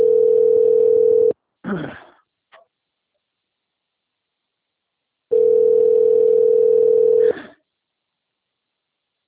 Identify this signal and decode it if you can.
Ringback tone (North America), 440 + 480 Hz.
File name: out- - - - - .2686